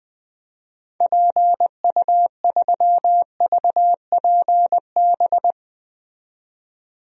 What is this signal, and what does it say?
Morse code, heard as a keyed tone, about 20 wpm, 700 Hz. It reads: PU3VPB
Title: DXCC-100 Call Signs - 6 Characters - 20wpm